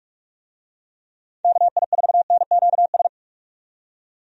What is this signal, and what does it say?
Morse code, heard as a keyed tone, about 46 wpm, 700 Hz. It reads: KI4NQS
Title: USA Call Signs - 6 Characters - 45wpm